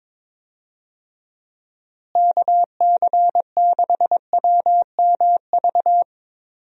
Morse code, heard as a keyed tone, about 22 wpm, 700 Hz. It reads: KC6WMV
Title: USA Call Signs - 6 Characters - Repeated 3x - 22wpm